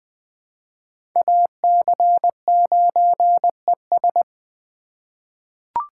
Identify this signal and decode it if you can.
Morse code, heard as a keyed tone, about 21 wpm, 700 Hz. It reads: AC9ES
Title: USA Call Signs - 5 Characters - 2x Character Spacing - 20wpm